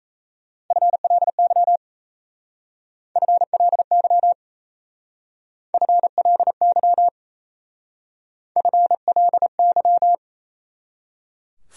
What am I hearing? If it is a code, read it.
Morse code, heard as a keyed tone, about 34 wpm, 700 Hz. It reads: FLY FLY FLY FLY
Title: Speed-Racing Top 500 Words 28wpm